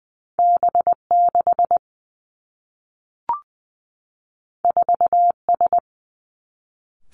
Morse code, heard as a keyed tone, about 20 wpm, 700 Hz. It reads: B 6 4 S
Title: Two Challenging Letter-Number Combinations 20wpm